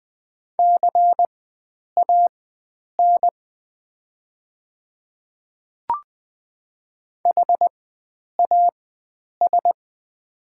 Morse code, heard as a keyed tone, about 20 wpm, spaced as for 9 wpm, 700 Hz. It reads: CAN HAS
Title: 4x Character Spacing - Top 100 Words - 20wpm